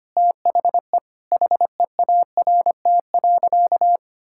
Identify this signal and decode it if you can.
Morse code, heard as a keyed tone, about 25 wpm, 700 Hz. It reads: THE HEART.
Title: Sentences from Top 1600 Words 25wpm